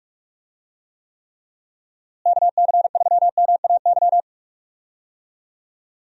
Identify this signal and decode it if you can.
Morse code, heard as a keyed tone, about 46 wpm, 700 Hz. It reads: KC3MAY
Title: USA Call Signs - 6 Characters - Repeated 3x - 45wpm